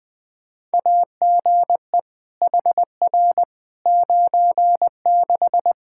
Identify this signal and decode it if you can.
Morse code, heard as a keyed tone, about 20 wpm, 700 Hz. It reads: AGE HR 96
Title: Practice QSO - Exchange 3 - More Info - 20wpm